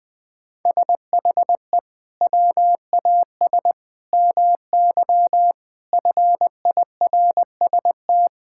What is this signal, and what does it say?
Morse code, heard as a keyed tone, about 20 wpm, 700 Hz. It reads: SHE WAS MY FIRST
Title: Sentences from Top 1000 Words - Review Part 6 - 20wpm